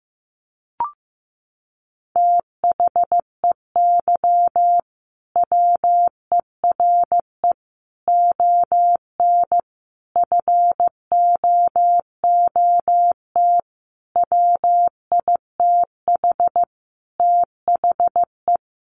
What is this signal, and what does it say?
Morse code, heard as a keyed tone, about 14 wpm, 700 Hz. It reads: THEY WERE ON FOOT WITH THE
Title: Sentences from Top 1500 Words 15wpm